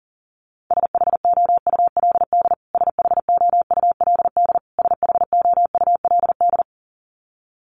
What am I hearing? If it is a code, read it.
Morse code, heard as a keyed tone, about 40 wpm, 700 Hz. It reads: SHOULD SHOULD SHOULD
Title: Top 300 Words Repeated 40wpm